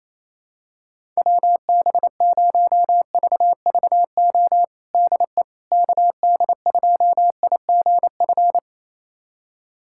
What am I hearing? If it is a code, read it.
Morse code, heard as a keyed tone, about 28 wpm, 700 Hz. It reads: WB0VVO DE KD2IGF
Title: USA Callsign DE Callsign - 6 Characters - 28wpm